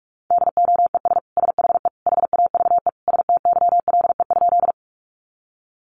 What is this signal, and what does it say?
Morse code, heard as a keyed tone, about 44 wpm, 700 Hz. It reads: DOES SHE HAVE STYLE?
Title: Sentences from Top 800 Words 45wpm